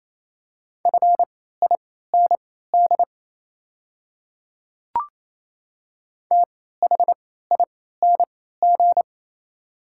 Morse code, heard as a keyed tone, about 27 wpm, spaced as for 16 wpm, 690 Hz. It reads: FIND THING
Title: 3x Character Spacing - Top 100 Words - 28wpm